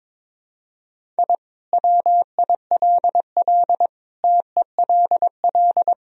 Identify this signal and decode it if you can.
Morse code, heard as a keyed tone, about 22 wpm, 700 Hz. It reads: I WILL TELL
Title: Sentences from Top 1000 Words - Review Part 1 - 22wpm